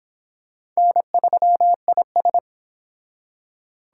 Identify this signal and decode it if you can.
Morse code, heard as a keyed tone, about 25 wpm, 700 Hz. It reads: N3IS